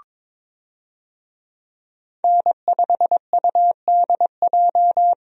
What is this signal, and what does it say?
Morse code, heard as a keyed tone, about 22 wpm, 700 Hz. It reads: N5UDJ
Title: USA Call Signs - 5 Characters - 22wpm